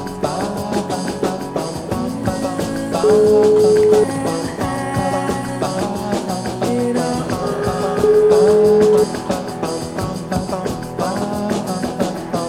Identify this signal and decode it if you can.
Ringback tone (Europe, 425 Hz), with noise.